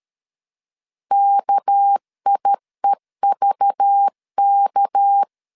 Morse code, heard as a keyed tone, about 13 wpm, 800 Hz. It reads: KIEVK
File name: LEZIONI / 6 PKX0 / 6 PK 12wpm 2006